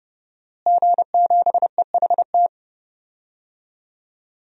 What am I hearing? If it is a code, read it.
Morse code, heard as a keyed tone, about 30 wpm, 700 Hz. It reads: G7EHT
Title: DXCC-100 Call Signs - 5 Characters - Repeated 2x - 30wpm